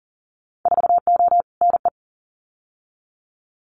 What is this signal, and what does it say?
Morse code, heard as a keyed tone, about 41 wpm, 700 Hz. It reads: 4O NE
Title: Winter Field Day Exchanges Encore 40wpm